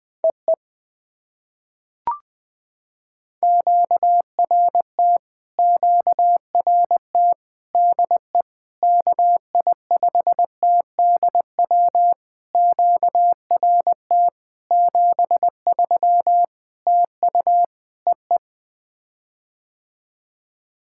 Morse code, heard as a keyed tone, about 20 wpm, 700 Hz. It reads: EE QRT QRT DE KI5TDW QRT 73 TU EE
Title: POTA QSO Practice - 20wpm